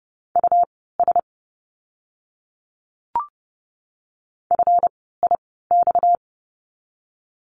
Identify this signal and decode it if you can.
Morse code, heard as a keyed tone, about 30 wpm, spaced as for 17 wpm, 700 Hz. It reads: US FIX
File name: course / 30:10wpm / 230 Lesson 37f - Letter x - Words 30wpm